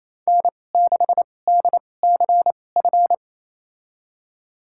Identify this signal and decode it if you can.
Morse code, heard as a keyed tone, about 28 wpm, spaced as for 20 wpm, 700 Hz. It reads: N6DCF
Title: USA Call Signs - 5 Characters - 2x Character Spacing - 28wpm